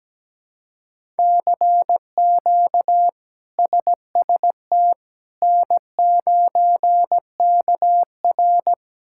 Morse code, heard as a keyed tone, about 17 wpm, 700 Hz. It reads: CQ SST N9KR
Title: K1USN SST QSO Practice - 17wpm